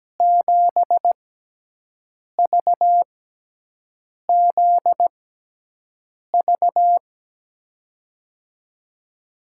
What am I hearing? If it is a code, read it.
Morse code, heard as a keyed tone, about 17 wpm, spaced as for 6 wpm, 700 Hz. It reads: 7VZV
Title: USA Call Signs - 6 Characters - 6x Character Spacing - 17wpm